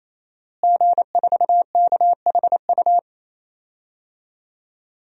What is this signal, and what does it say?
Morse code, heard as a keyed tone, about 27 wpm, 700 Hz. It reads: G4KHU
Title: DXCC-100 Call Signs - 5 Characters - Repeated 3x - 28wpm